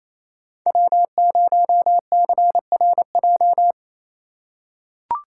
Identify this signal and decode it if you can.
Morse code, heard as a keyed tone, about 28 wpm, 700 Hz. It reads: W0CRJ
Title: USA Call Signs - 2x Character Spacing - 28wpm